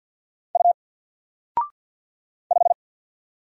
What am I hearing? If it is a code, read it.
Morse code, heard as a keyed tone, about 50 wpm, 700 Hz. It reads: U 5
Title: Single Letter-Number Alt 50wpm